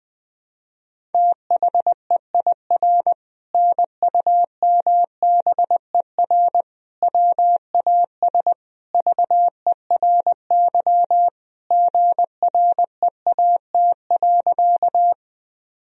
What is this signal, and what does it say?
Morse code, heard as a keyed tone, about 20 wpm, 700 Hz. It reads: THEIR NUMBER WAS VERY GREAT.
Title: Sentences from Top 1000 Words - Review Part 1 - 20wpm